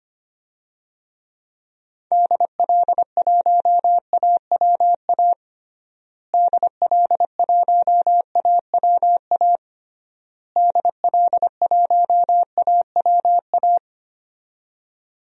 Morse code, heard as a keyed tone, about 25 wpm, 700 Hz. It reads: DL1AWA DL1AWA DL1AWA
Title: DXCC-100 Call Signs - 6 Characters - Repeated 3x - 25wpm